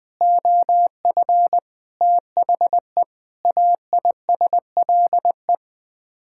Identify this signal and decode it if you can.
Morse code, heard as a keyed tone, about 21 wpm, 700 Hz. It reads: OF THE AISLE
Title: Sets of 6 Words - 20wpm